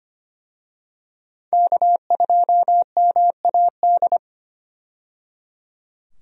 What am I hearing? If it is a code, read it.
Morse code, heard as a keyed tone, about 25 wpm, 690 Hz. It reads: K2MAD
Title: USA Call Signs - 5 Characters - 25wpm